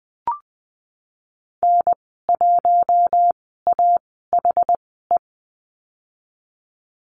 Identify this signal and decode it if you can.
Morse code, heard as a keyed tone, about 20 wpm, spaced as for 15 wpm, 700 Hz. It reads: N1AHE
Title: 147 Lesson 24g - Letter y - Call Signs 20wpm